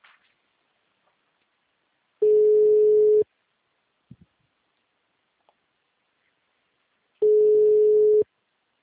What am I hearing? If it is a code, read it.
Ringback tone (Europe, 425 Hz).